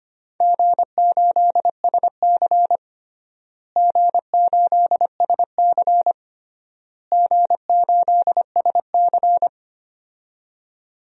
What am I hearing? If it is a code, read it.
Morse code, heard as a keyed tone, about 25 wpm, 700 Hz. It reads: G8SC G8SC G8SC
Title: DXCC-100 Call Signs - 4 Characters - Repeated 3x - 25wpm